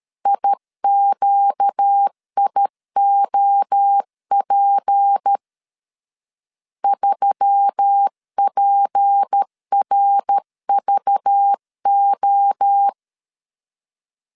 Morse code, heard as a keyed tone, about 13 wpm, 800 Hz. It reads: IQIOP 3PRVO
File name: CW / DET / 1 6 1500C S C 12wpm 3533